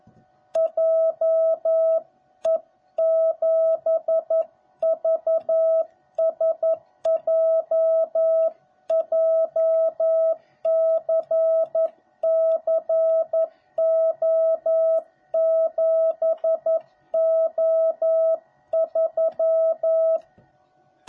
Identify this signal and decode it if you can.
Morse code, heard as a keyed tone, about 11 wpm, 660 Hz. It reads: JE7VSJ JCCO7O3